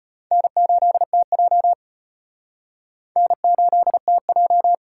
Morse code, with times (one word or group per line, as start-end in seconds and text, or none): 0.3-1.7 N8TJ
3.2-4.7 N8TJ